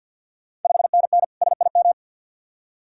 Morse code, heard as a keyed tone, about 51 wpm, 690 Hz. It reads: 5NN SIM
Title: Ontario QSO Party Exchange 50wpm